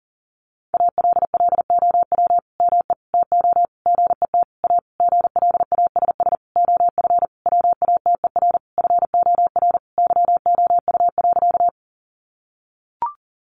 Morse code, heard as a keyed tone, about 40 wpm, 700 Hz. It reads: ALLOW ME TO GET A GLASS OF WATER FOR YOU.
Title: Sentences from Top 1300 Words 40wpm